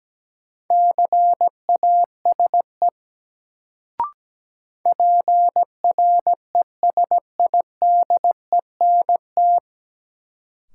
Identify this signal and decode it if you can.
Morse code, heard as a keyed tone, about 17 wpm, 700 Hz. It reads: CASE PRESIDENT
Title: Sets of 2 Words - New Word Drill - 17wpm